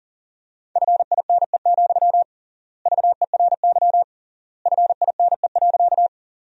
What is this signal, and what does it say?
Morse code, heard as a keyed tone, about 40 wpm, 700 Hz. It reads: FINE, VERY FINE.
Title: 3x Word-Spacing - Sentences from Top 500 Words - 40wpm